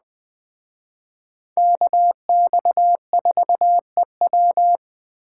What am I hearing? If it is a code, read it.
Morse code, heard as a keyed tone, about 20 wpm, 700 Hz. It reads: KX4EW